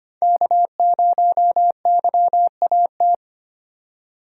Morse code, heard as a keyed tone, about 25 wpm, 700 Hz. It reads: K0YAT
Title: USA Call Signs - 5 Characters - 3x Character Spacing - 25wpm